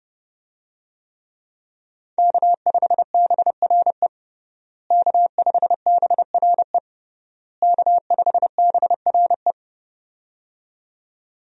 Morse code, heard as a keyed tone, about 32 wpm, 700 Hz. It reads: K5BRE K5BRE K5BRE
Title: USA Call Signs - 5 Characters - Repeated 3x - 30wpm